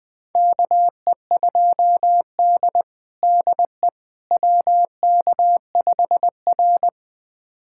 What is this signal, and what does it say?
Morse code, heard as a keyed tone, about 21 wpm, 700 Hz. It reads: KE2D DE WK5R
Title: USA Callsign DE Callsign - 4 Characters - 20wpm